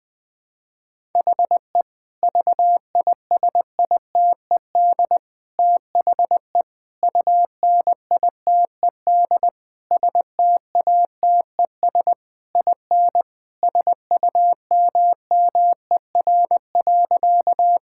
Morse code, heard as a keyed tone, about 21 wpm, 700 Hz. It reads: HE VISITED THE UNITED STATES IN SUMMER.
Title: Sentences with 4-Word Phrases - 20wpm